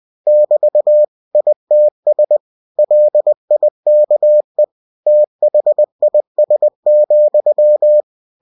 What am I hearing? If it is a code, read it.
Morse code, heard as a keyed tone, about 20 wpm, 600 Hz. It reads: <BT> ITS LIKE THIS,
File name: cw-books / winnie-the-pooh / winnie-20wpm-page-0005